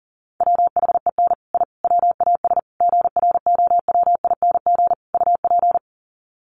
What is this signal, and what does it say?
Morse code, heard as a keyed tone, about 38 wpm, 700 Hz. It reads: WHEN I WAS GROWING UP
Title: Sets of 5 Words Encore 40wpm